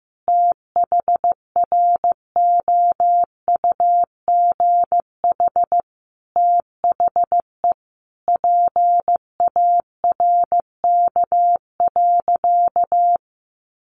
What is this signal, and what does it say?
Morse code, heard as a keyed tone, about 15 wpm, 700 Hz. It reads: THROUGH THE PARK.
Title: Sentences from Top 1300 Words 15wpm